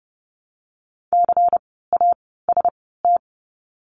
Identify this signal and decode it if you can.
Morse code, heard as a keyed tone, about 31 wpm, 700 Hz. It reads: C A S T
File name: Lesson 16f - Letter c - Words 30wpm